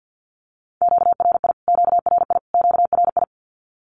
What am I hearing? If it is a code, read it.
Morse code, heard as a keyed tone, about 48 wpm, 700 Hz. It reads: QRI QRI QRI
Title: Q-Codes Repeated 3x 50wpm